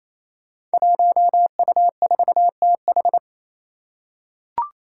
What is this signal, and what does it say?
Morse code, heard as a keyed tone, about 27 wpm, 700 Hz. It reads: 1U4TH
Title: Five Challenging Letter-Number Combinations 28wpm